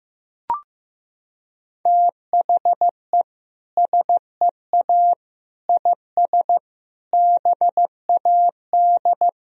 Morse code, heard as a keyed tone, about 15 wpm, 700 Hz. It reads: THE SEA IS BAD
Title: Sentences from Top 1000 Words - Review Part 1 - 15wpm